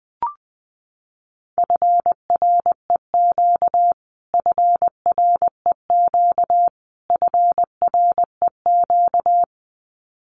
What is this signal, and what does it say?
Morse code, heard as a keyed tone, about 20 wpm, 700 Hz. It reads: FREQ FREQ FREQ
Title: Abbreviations Repeated 3x - 20wpm